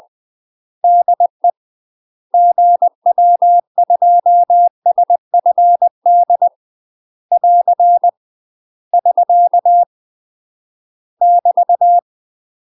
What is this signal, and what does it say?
Morse code, heard as a keyed tone, about 20 wpm, 700 Hz. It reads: DE GW2SFD <AR> <SK> <BT>
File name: 20WPM